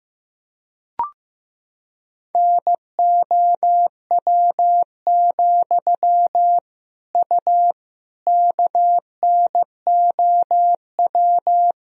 Morse code, heard as a keyed tone, about 16 wpm, 700 Hz. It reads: NOW, U KNOW
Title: Sentences from Top 200 Words and Abbreviations 15wpm